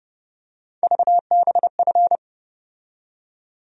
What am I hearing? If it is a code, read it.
Morse code, heard as a keyed tone, about 30 wpm, 700 Hz. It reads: VBF
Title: USA Call Sign Postfixes - 3 Letters 30wpm